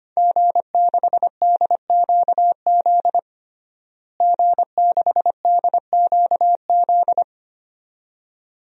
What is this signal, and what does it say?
Morse code, heard as a keyed tone, about 25 wpm, 700 Hz. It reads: G6DQZ G6DQZ
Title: DXCC-100 Call Signs - 5 Characters - Repeated 2x - 25wpm